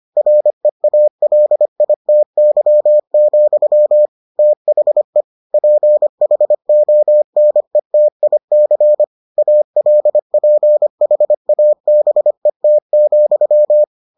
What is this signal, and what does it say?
Morse code, heard as a keyed tone, about 25 wpm, 600 Hz. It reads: REALITY, THE PHONETIC ALPHABET,